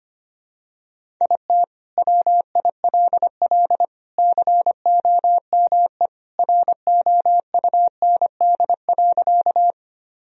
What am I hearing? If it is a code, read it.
Morse code, heard as a keyed tone, about 25 wpm, 700 Hz. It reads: IT WILL COME ROUND.
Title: Sentences from Top 1000 Words - Review Part 1 - 25wpm